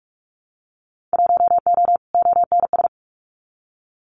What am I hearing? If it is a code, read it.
Morse code, heard as a keyed tone, about 46 wpm, 700 Hz. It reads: 1O ONS
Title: Winter Field Day Exchanges Encore 45wpm